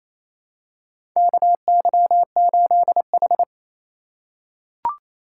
Morse code, heard as a keyed tone, about 28 wpm, 700 Hz. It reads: KY8H
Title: USA Call Signs - 4 Characters - 2x Character Spacing - 28wpm